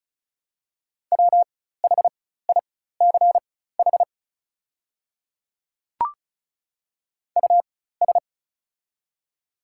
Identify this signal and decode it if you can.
Morse code, heard as a keyed tone, about 36 wpm, spaced as for 16 wpm, 690 Hz. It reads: WHICH US
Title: 4x Character Spacing - Top 200 Words - 35wpm